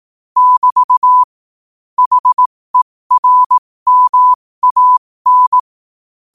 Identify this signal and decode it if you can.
Morse code, heard as a keyed tone, about 18 wpm, spaced as for 15 wpm, 1000 Hz. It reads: <BT> HERMAN
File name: qotdfiles / 15wpm